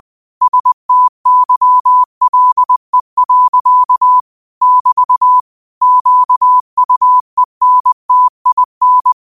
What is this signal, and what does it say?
Morse code, heard as a keyed tone, about 20 wpm, 1000 Hz. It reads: STYLE. <BT> QUENTIN
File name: qotdfiles / 20wpm